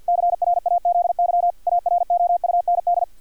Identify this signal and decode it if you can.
Morse code, heard as a keyed tone, about 50 wpm, 700 Hz, with radio noise. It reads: CRAZY AROUND
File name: archived / CRSnet